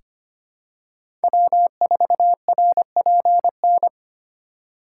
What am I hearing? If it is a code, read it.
Morse code, heard as a keyed tone, about 25 wpm, 700 Hz. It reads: W4RPN
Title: USA Call Signs - 5 Characters - 2x Character Spacing - 25wpm